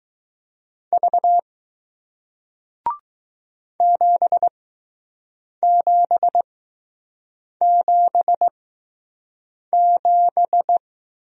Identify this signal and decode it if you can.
Morse code, heard as a keyed tone, about 20 wpm, spaced as for 6 wpm, 700 Hz. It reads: V 7777